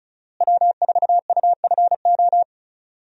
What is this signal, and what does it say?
Morse code, heard as a keyed tone, about 36 wpm, 700 Hz. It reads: W4UFO
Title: USA Call Signs - 5 Characters - Repeated 2x - 35wpm